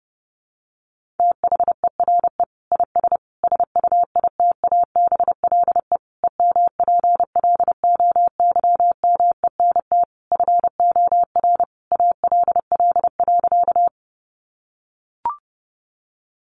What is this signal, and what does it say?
Morse code, heard as a keyed tone, about 30 wpm, 700 Hz. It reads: THERE IS SUITABLE EMPLOYMENT FOR ALL.
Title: Sentences from Top 1500 Words 30wpm